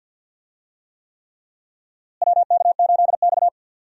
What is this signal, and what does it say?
Morse code, heard as a keyed tone, about 47 wpm, 700 Hz. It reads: WK8X
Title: USA Call Signs - 4 Characters - Repeated 3x - 50wpm